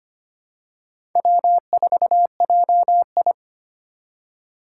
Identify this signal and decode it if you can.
Morse code, heard as a keyed tone, about 25 wpm, 700 Hz. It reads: W4JI
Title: USA Call Signs - 2x Character Spacing - 25wpm